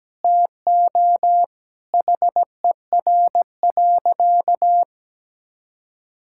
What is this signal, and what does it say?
Morse code, heard as a keyed tone, about 17 wpm, 700 Hz. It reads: TO HER.
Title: Sentences from Top 1000 Words - Review Part 6 - 17wpm